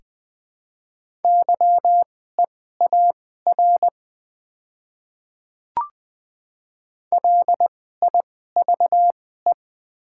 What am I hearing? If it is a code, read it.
Morse code, heard as a keyed tone, about 20 wpm, spaced as for 14 wpm, 700 Hz. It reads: YEAR LIVE
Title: 2x Character Spacing - Top 200 Words - 20wpm